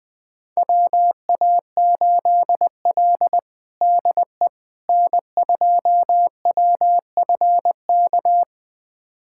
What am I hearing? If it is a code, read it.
Morse code, heard as a keyed tone, about 20 wpm, 700 Hz. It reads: WA8L DE N2WFK
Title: USA Callsign DE Callsign 20wpm